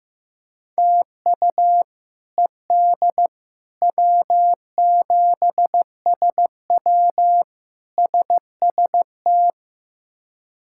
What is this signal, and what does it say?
Morse code, heard as a keyed tone, about 15 wpm, 700 Hz. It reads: TU ED W7SW SST
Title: K1USN SST QSO Practice - 15wpm